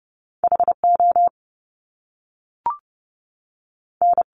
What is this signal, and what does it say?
Morse code, heard as a keyed tone, about 30 wpm, spaced as for 3 wpm, 700 Hz. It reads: H ON
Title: Top Two Letter Combinations in Words 30wpm